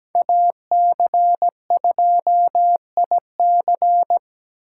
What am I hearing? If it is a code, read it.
Morse code, heard as a keyed tone, about 20 wpm, 700 Hz. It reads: AC2IC
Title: Speed Racing - USA 2x2 Call Signs 17wpm